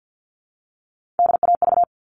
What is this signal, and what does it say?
Morse code, heard as a keyed tone, about 50 wpm, 700 Hz. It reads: DAV
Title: All Cut Numbers - 3-Digit 50wpm